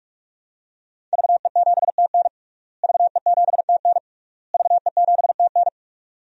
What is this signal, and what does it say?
Morse code, heard as a keyed tone, about 41 wpm, 700 Hz. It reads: VE7TN VE7TN VE7TN
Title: DXCC-100 Call Signs - 5 Characters - Repeated 3x - 45wpm